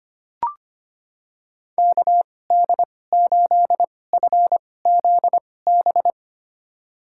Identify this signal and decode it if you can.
Morse code, heard as a keyed tone, about 25 wpm, spaced as for 18 wpm, 700 Hz. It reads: KD8FZB
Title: USA Call Signs - 6 Characters - 2x Character Spacing - 25wpm